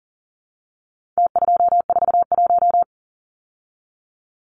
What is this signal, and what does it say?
Morse code, heard as a keyed tone, about 40 wpm, 700 Hz. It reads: T241
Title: Numbers - Contest Style - 4-Digit 40wpm